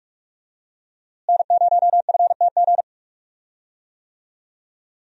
Morse code, heard as a keyed tone, about 42 wpm, 700 Hz. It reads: N0FTG
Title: USA Call Signs - 5 Characters - Repeated 2x - 45wpm